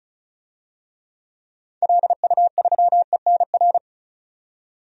Morse code, heard as a keyed tone, about 34 wpm, 700 Hz. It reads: LU3ENR
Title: DXCC-100 Call Signs - 6 Characters - 35wpm